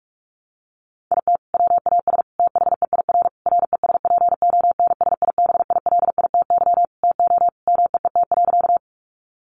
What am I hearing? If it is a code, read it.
Morse code, heard as a keyed tone, about 44 wpm, 700 Hz. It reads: IT WAS THEIR RESPONSIBILITY TO MEET.
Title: Sentences from Top 1200 Words 45wpm